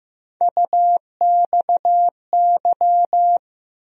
Morse code, heard as a keyed tone, about 15 wpm, 700 Hz. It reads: UXY
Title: USA Call Sign Postfixes - 3 Letters 15wpm